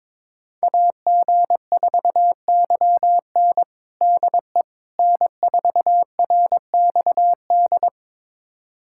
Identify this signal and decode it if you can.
Morse code, heard as a keyed tone, about 22 wpm, 700 Hz. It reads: AG4YN DE N4RXD
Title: USA Callsign DE Callsign - 5 Characters - 22wpm